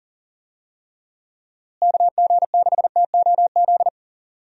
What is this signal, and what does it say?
Morse code, heard as a keyed tone, about 40 wpm, 700 Hz. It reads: KG6TOZ
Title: USA Call Signs - 6 Characters - Repeated 2x - 40wpm